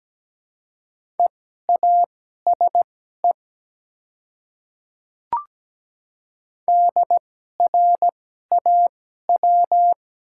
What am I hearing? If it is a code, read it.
Morse code, heard as a keyed tone, about 17 wpm, spaced as for 12 wpm, 690 Hz. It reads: EASE DRAW
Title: 2x Character Spacing - Top 300 Words - 17wpm